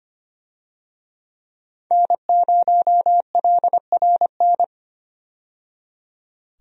Morse code, heard as a keyed tone, about 25 wpm, 700 Hz. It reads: N0LRN
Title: USA Call Signs - 5 Characters - 25wpm